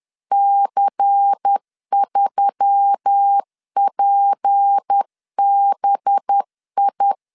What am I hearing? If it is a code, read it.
Morse code, heard as a keyed tone, about 10 wpm, 800 Hz. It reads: C3PBI
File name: LEZIONI / 7 GF9Z / 7 GF 10wpm 2514